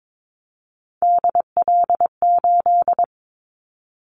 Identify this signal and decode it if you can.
Morse code, heard as a keyed tone, about 22 wpm, 700 Hz. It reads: DL8
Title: Callsign Prefixes for Top 50 DXCC - 22wpm